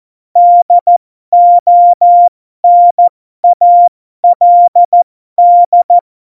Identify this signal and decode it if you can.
Morse code, heard as a keyed tone, about 14 wpm, spaced as for 12 wpm, 700 Hz. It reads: DONALD
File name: Morsekurs / Witze